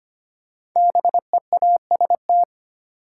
Morse code, heard as a keyed tone, about 25 wpm, 700 Hz. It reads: BEAST
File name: 5 Letter Words - First 3 Letters Given - 25wpm